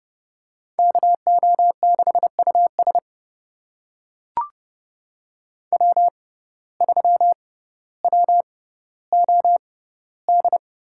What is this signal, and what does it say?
Morse code, heard as a keyed tone, about 32 wpm, spaced as for 16 wpm, 700 Hz. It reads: KO6US W3WOD
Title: USA Call Signs - 5 Characters - 6x Character Spacing - 30wpm